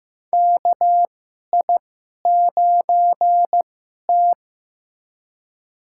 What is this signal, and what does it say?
Morse code, heard as a keyed tone, about 15 wpm, spaced as for 11 wpm, 700 Hz. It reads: KI9T
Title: USA Call Signs - 2x Character Spacing - 15wpm